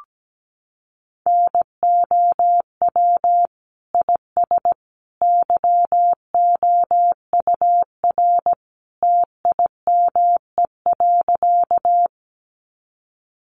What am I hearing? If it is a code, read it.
Morse code, heard as a keyed tone, about 17 wpm, 700 Hz. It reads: NOW IS YOUR TIME.
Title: Sentences from Top 100 Words 17wpm